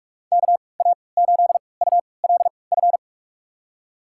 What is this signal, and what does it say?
Morse code, heard as a keyed tone, about 46 wpm, spaced as for 25 wpm, 700 Hz. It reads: KA8ULF
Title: USA Call Signs - 6 Characters - 3x Character Spacing - 45wpm